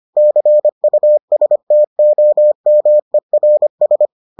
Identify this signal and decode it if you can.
Morse code, heard as a keyed tone, about 25 wpm, 600 Hz. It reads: CUSTOMERS